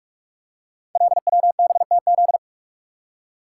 Morse code, heard as a keyed tone, about 44 wpm, 700 Hz. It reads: LWBTZ
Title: Code Groups - 5 Letter:Numbers - 45wpm